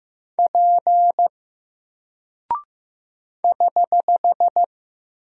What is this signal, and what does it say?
Morse code, heard as a keyed tone, about 15 wpm, 700 Hz. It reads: P <HH>
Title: Single Character (Letter:Number:Punctuation:Prosign) 15wpm